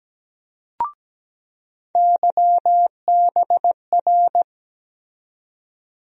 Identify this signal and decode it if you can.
Morse code, heard as a keyed tone, about 17 wpm, 700 Hz. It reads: YBR
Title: USA Call Sign Postfixes - 3 Letters 17wpm